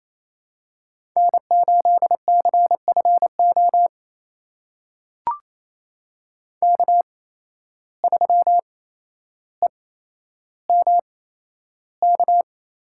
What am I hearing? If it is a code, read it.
Morse code, heard as a keyed tone, about 28 wpm, spaced as for 12 wpm, 700 Hz. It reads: N8CFO K3EMK
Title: USA Call Signs - 5 Characters - 8x Character Spacing - 28wpm